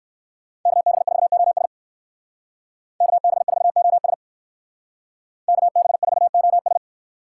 Morse code, heard as a keyed tone, about 62 wpm, 700 Hz. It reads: KB4OS KB4OS KB4OS